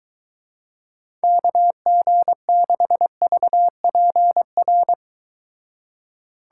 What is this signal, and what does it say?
Morse code, heard as a keyed tone, about 23 wpm, 700 Hz. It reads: KG6VPR